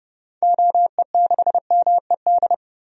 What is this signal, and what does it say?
Morse code, heard as a keyed tone, about 32 wpm, 700 Hz. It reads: OE6MED
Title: DXCC-100 Call Signs - 6 Characters - 30wpm